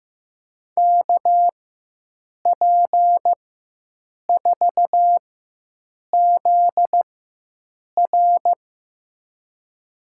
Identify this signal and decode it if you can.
Morse code, heard as a keyed tone, about 15 wpm, spaced as for 7 wpm, 700 Hz. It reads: KP4ZR
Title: USA Call Signs - 5 Characters - 4x Character Spacing - 15wpm